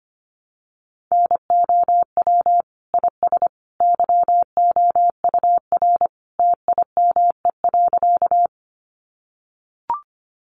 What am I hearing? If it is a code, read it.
Morse code, heard as a keyed tone, about 25 wpm, 700 Hz. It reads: NOW IS YOUR TIME.
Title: Sentences from Top 100 Words 25wpm